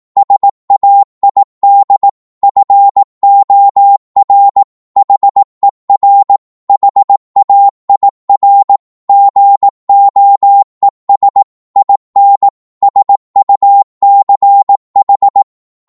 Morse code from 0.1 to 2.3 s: SAID